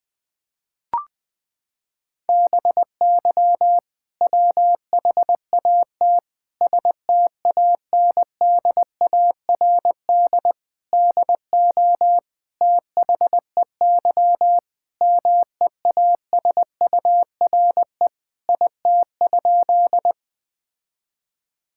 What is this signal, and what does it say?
Morse code, heard as a keyed tone, about 20 wpm, 700 Hz. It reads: BY WHAT STANDARD DO THEY MEASURE IT?
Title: Sentences from Top 1000 Words - Review Part 6 - 20wpm